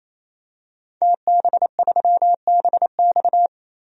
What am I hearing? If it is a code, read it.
Morse code, heard as a keyed tone, about 28 wpm, 700 Hz. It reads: TB3BX
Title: DXCC-100 Call Signs - 5 Characters - 28wpm